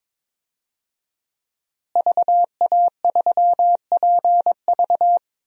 Morse code, heard as a keyed tone, about 22 wpm, 700 Hz. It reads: VA3PV
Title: DXCC-100 Call Signs - 5 Characters - Repeated 3x - 22wpm